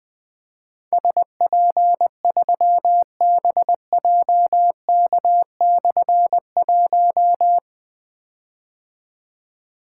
Morse code, heard as a keyed tone, about 20 wpm, 700 Hz. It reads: SP3BJK/1
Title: Difficult Call Signs 20wpm